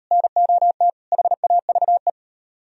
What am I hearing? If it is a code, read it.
Morse code, heard as a keyed tone, about 37 wpm, 700 Hz. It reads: NOT HAVE